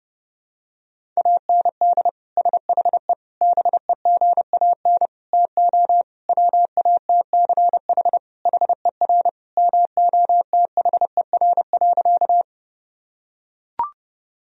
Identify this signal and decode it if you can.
Morse code, heard as a keyed tone, about 32 wpm, 700 Hz. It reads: AND SHE BEGAN TO WATCH HER MOTHER.
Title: Sentences from Top 1000 Words - Review Part 1 - 30wpm